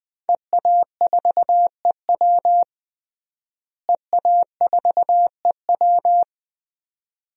Morse code, heard as a keyed tone, about 20 wpm, 700 Hz. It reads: EA4EW EA4EW
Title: DXCC-100 Call Signs - 5 Characters - Repeated 2x - 20wpm